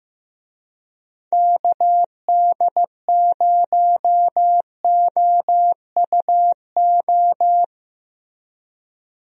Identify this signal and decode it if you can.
Morse code, heard as a keyed tone, about 15 wpm, 700 Hz. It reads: KD0OUO
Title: USA Call Signs - 6 Characters - Repeated 3x - 15wpm